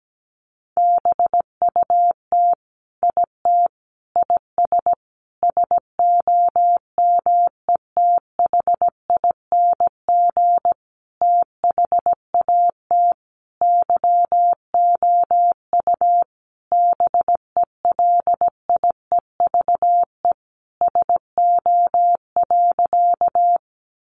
Morse code, heard as a keyed tone, about 17 wpm, 700 Hz. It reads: BUT IT IS SOMETHING THAT YOU BELIEVE SO.
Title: Sentences from Top 1000 Words 17wpm